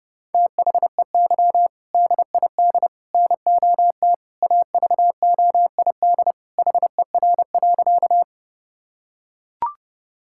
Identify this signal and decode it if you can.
Morse code, heard as a keyed tone, about 30 wpm, 700 Hz. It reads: THEY DID NOT AVOID HER.
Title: Sentences from Top 1000 Words - Review Part 6 - 30wpm